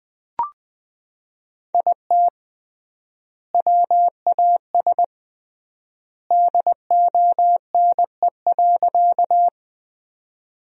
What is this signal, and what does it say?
Morse code, heard as a keyed tone, about 20 wpm, 700 Hz. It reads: IT WAS DONE.
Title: 3x Word-Spacing - Sentences from Top 500 Words - 20wpm